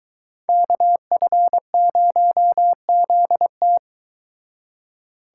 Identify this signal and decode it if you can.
Morse code, heard as a keyed tone, about 22 wpm, 700 Hz. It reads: KF0ZT